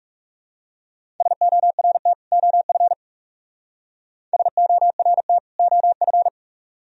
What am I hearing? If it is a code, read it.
Morse code, heard as a keyed tone, about 37 wpm, 700 Hz. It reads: SORT OF SORT OF
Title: Speed-Racing - Sets of 2 Words 30wpm